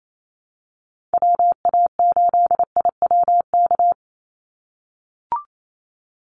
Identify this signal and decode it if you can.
Morse code, heard as a keyed tone, about 28 wpm, 700 Hz. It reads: WA8IWK
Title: Call Signs - 28wpm